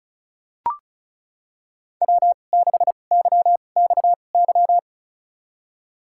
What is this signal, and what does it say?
Morse code, heard as a keyed tone, about 36 wpm, spaced as for 25 wpm, 700 Hz. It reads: W6YXY
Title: USA Call Signs - 5 Characters - 2x Character Spacing - 35wpm